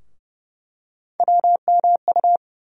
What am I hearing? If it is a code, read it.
Morse code, heard as a keyed tone, about 30 wpm, 700 Hz. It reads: WMU
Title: USA Call Sign Postfixes - 3 Letters 30wpm